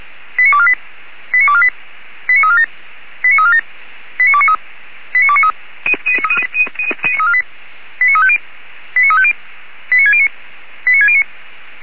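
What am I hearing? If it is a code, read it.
Five-tone selective call (ZVEI), heard as quick runs of five tones, tones of about 70 ms. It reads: 89248 89248 89347 89347 89293 89293 89350 89350 89609 89609
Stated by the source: ZVEI five-tone call received by SDR, archived by Feuerwehr Orlamünde (fire brigade).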